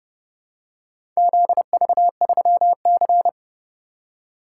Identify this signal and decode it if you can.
Morse code, heard as a keyed tone, about 30 wpm, 700 Hz. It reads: ZV3C